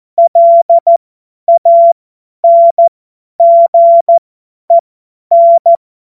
Morse code, heard as a keyed tone, about 14 wpm, spaced as for 10 wpm, 680 Hz. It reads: LANGEN
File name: Morsekurs / Witze / Witz 06 Februar 2024 50 F70